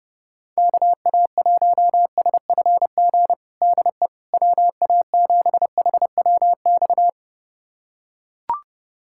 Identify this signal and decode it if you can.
Morse code, heard as a keyed tone, about 30 wpm, 700 Hz. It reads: KA1SFG DE WA7HWX
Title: USA Callsign DE Callsign 30wpm